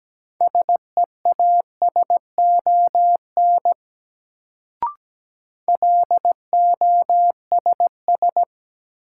Morse code, heard as a keyed tone, about 17 wpm, 700 Hz. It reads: SEASON LOSS
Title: Sentences from Top 1400 Words - New Word Drill Alt - 17wpm